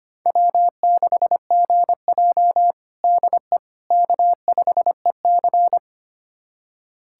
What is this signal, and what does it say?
Morse code, heard as a keyed tone, about 25 wpm, 700 Hz. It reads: W6GJ DE K5EC
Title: USA Callsign DE Callsign - 4 Characters - 25wpm